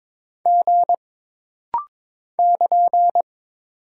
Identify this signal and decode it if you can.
Morse code, heard as a keyed tone, about 22 wpm, 700 Hz. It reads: G <KN>
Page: This rapid-fire timing is geared to push your ICR (Instant Character Recognition) skills.